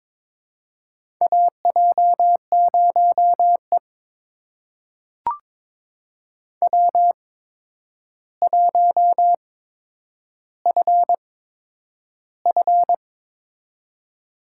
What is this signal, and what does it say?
Morse code, heard as a keyed tone, about 22 wpm, spaced as for 9 wpm, 700 Hz. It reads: AJ0E W1FF
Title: USA Call Signs - 4 Characters - 8x Character Spacing - 22wpm